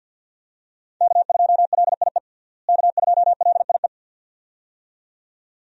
Morse code, heard as a keyed tone, about 50 wpm, 700 Hz. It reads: K2LIE K2LIE
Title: USA Call Signs - 5 Characters - Repeated 2x - 50wpm